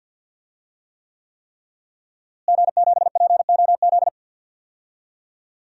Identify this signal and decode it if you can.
Morse code, heard as a keyed tone, about 53 wpm, 700 Hz. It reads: G7POZ
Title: DXCC-100 Call Signs - 5 Characters - Repeated 3x - 50wpm